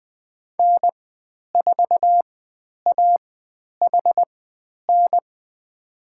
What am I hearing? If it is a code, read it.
Morse code, heard as a keyed tone, about 20 wpm, spaced as for 10 wpm, 700 Hz. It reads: N4AHN
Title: 109 Lesson 18g - Letter m - Call Signs 20wpm